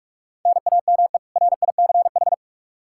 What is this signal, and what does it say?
Morse code, heard as a keyed tone, about 46 wpm, 700 Hz. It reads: NAME RICH
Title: K1USN SST QSO Practice 45wpm